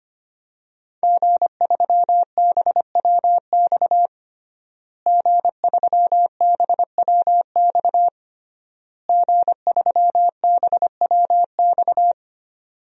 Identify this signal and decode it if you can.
Morse code, heard as a keyed tone, about 25 wpm, 700 Hz. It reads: G3BWX G3BWX G3BWX
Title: DXCC-100 Call Signs - 5 Characters - Repeated 3x - 25wpm